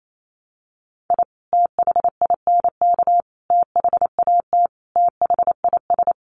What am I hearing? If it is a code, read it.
Morse code, heard as a keyed tone, about 28 wpm, 700 Hz. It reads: I THINK THAT THIS
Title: Sets of 5 Words Encore2 28wpm